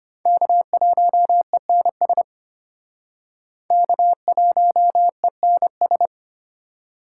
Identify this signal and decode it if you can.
Morse code, heard as a keyed tone, about 32 wpm, 700 Hz. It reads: K1ENS K1ENS